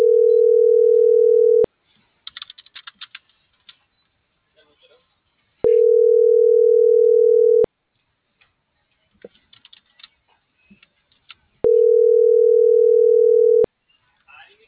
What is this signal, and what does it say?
Ringback tone (North America), 440 + 480 Hz.